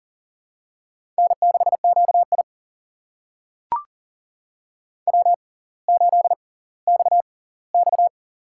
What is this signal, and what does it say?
Morse code, heard as a keyed tone, about 38 wpm, spaced as for 20 wpm, 700 Hz. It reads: N6QI W8XX
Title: USA Call Signs - 4 Characters - 6x Character Spacing - 40wpm